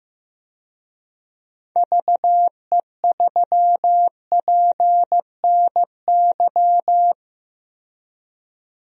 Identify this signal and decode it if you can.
Morse code, heard as a keyed tone, about 15 wpm, 700 Hz. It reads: VE3PNY
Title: DXCC-100 Call Signs - 6 Characters - 15wpm